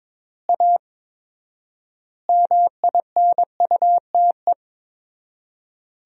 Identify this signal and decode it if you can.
Morse code, heard as a keyed tone, about 22 wpm, 700 Hz. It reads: A MINUTE
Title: 4x Word-Spacing - Sentences from Top 500 Words - 22wpm